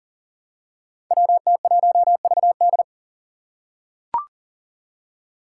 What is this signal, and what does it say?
Morse code, heard as a keyed tone, about 42 wpm, 700 Hz. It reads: WT1VD
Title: Five Challenging Letter-Number Combinations 40wpm